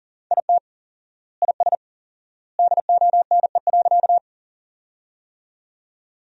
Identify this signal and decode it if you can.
Morse code, heard as a keyed tone, about 40 wpm, 700 Hz. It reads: IT IS DONE.
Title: 4x Word-Spacing - Sentences from Top 500 Words - 40wpm